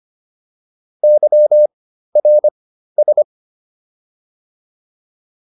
Morse code, heard as a keyed tone, about 25 wpm, spaced as for 13 wpm, 600 Hz. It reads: YRS